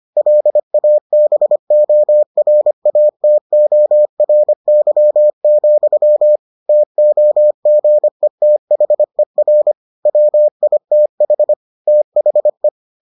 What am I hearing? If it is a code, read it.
Morse code, heard as a keyed tone, about 25 wpm, 600 Hz. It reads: LABORATORY, TOGETHER WITH THE